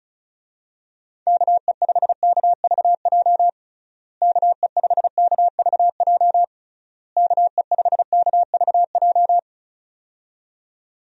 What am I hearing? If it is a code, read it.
Morse code, heard as a keyed tone, about 34 wpm, 700 Hz. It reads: KE5KVJ KE5KVJ KE5KVJ
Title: USA Call Signs - 6 Characters - Repeated 3x - 35wpm